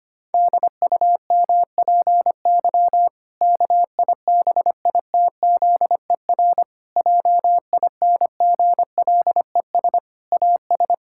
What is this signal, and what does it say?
Morse code, heard as a keyed tone, about 25 wpm, 700 Hz. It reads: DUMPY KIBITZER JINGLES AS
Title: Pangram Challenge v2 - 25wpm